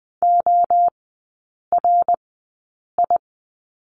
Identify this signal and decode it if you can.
Morse code, heard as a keyed tone, about 20 wpm, spaced as for 8 wpm, 700 Hz. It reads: ORI
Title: 053 Lesson 10e - Letter r - Three Characters 20wpm